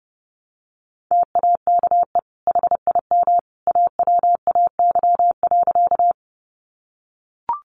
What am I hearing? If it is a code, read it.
Morse code, heard as a keyed tone, about 28 wpm, 700 Hz. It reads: TAKE HIM AWAY.
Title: Sentences from Top 600 Words 30wpm